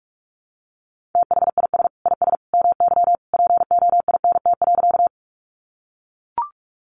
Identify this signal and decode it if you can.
Morse code, heard as a keyed tone, about 44 wpm, 700 Hz. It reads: THIS IS MY POINT.
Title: Sentences from Top 200 Words 45wpm